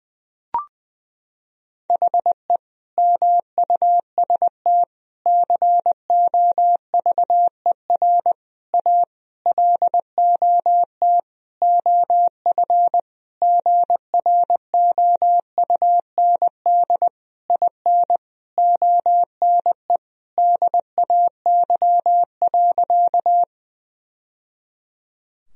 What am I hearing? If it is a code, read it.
Morse code, heard as a keyed tone, about 20 wpm, 700 Hz. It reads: HE MUST COVER A LOT OF GROUND IN ONE DAY.
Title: Sentences from Top 1000 Words - Review Part 3 - 20wpm